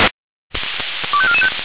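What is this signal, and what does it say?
Five-tone selective call (ZVEI), heard as quick runs of five tones, tones of about 70 ms. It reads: 25445
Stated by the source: ZVEI five-tone selective call 25419 received by SDR.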